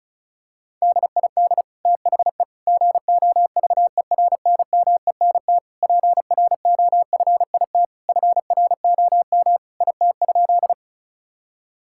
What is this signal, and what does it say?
Morse code, heard as a keyed tone, about 36 wpm, 700 Hz. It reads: DID THE GOVERNMENT PROFIT FROM IT?
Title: Sentences from Top 1000 Words - Review Part 5 - 35wpm